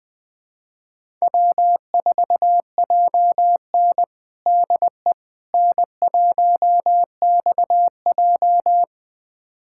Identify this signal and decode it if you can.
Morse code, heard as a keyed tone, about 20 wpm, 700 Hz. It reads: W4JN DE N1XJ
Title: USA Callsign DE Callsign - 4 Characters - 20wpm